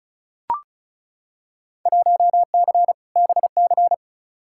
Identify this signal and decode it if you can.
Morse code, heard as a keyed tone, about 34 wpm, 700 Hz. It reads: 1C BC
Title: Field Day Exchanges Encore - 35wpm